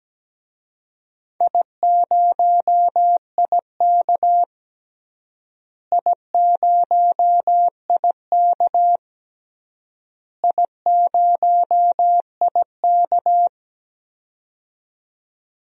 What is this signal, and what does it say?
Morse code, heard as a keyed tone, about 17 wpm, 700 Hz. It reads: I0IK I0IK I0IK
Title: DXCC-100 Call Signs - 4 Characters - Repeated 3x - 17wpm